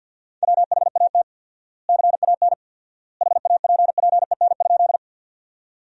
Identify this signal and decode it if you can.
Morse code, heard as a keyed tone, about 53 wpm, 700 Hz. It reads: WHAT CAN HAPPEN?
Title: 4x Word-Spacing - Sentences from Top 400 Words - 50wpm